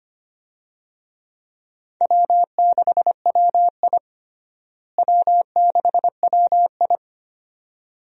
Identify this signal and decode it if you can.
Morse code, heard as a keyed tone, about 25 wpm, 700 Hz. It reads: W6WI W6WI
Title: USA Call Signs - 4 Characters - Repeated 2x - 25wpm